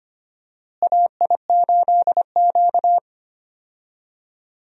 Morse code, heard as a keyed tone, about 25 wpm, 700 Hz. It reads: AI8Q
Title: USA Call Signs - 4 Characters - Repeated 3x - 25wpm